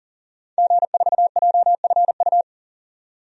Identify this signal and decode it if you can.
Morse code, heard as a keyed tone, about 38 wpm, 700 Hz. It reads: G4JFU
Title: DXCC-100 Call Signs - 5 Characters - 40wpm